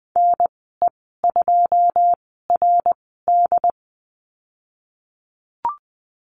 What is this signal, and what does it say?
Morse code, heard as a keyed tone, about 20 wpm, spaced as for 15 wpm, 700 Hz. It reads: NE2RD
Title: 082 Lesson 14g - Number 2 - Call Signs 20wpm